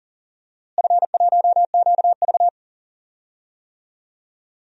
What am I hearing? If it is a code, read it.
Morse code, heard as a keyed tone, about 37 wpm, 700 Hz. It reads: F1QV
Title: DXCC-100 Call Signs - 4 Characters - Repeated 3x - 40wpm